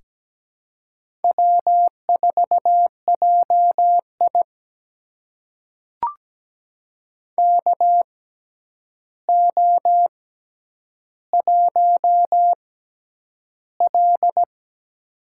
Morse code, heard as a keyed tone, about 17 wpm, spaced as for 9 wpm, 700 Hz. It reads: W4JI KO1L
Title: USA Call Signs - 4 Characters - 6x Character Spacing - 17wpm